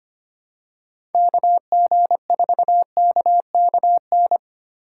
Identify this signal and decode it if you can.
Morse code, heard as a keyed tone, about 25 wpm, 700 Hz. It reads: KG4KKN
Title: USA Call Signs - 6 Characters - 2x Character Spacing - 25wpm